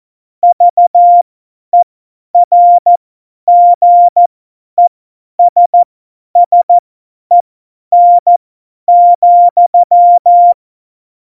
Morse code, heard as a keyed tone, about 14 wpm, spaced as for 10 wpm, 700 Hz. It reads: VERGESSEN,
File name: Morsekurs / Witze / Witz 01. Oktober 2019, 50 F70